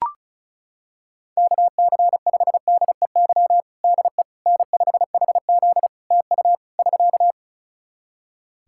Morse code, heard as a keyed tone, about 36 wpm, 700 Hz. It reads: KC5DEY DE N5HZ TU <SK>
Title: Practice QSO - Exchange 4 - Ending - 35wpm